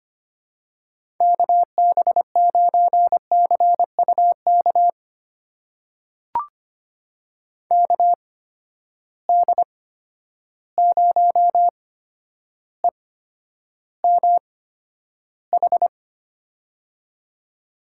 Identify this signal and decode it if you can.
Morse code, heard as a keyed tone, about 25 wpm, spaced as for 11 wpm, 700 Hz. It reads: KB9CUK KD0EMH
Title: USA Call Signs - 6 Characters - 8x Character Spacing - 25wpm